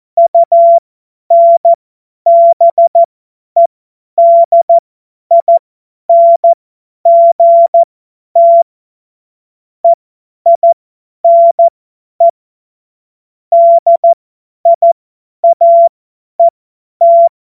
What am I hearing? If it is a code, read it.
Morse code, heard as a keyed tone, about 14 wpm, spaced as for 10 wpm, 680 Hz. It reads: UNBEDINGT EINE DIAET
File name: Morsekurs / Witze / Witz 19 März 2024 50 F70